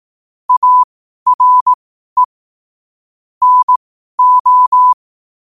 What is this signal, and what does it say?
Morse code, heard as a keyed tone, about 18 wpm, spaced as for 12 wpm, 1000 Hz. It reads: ARE NO
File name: qotdfiles / 12wpm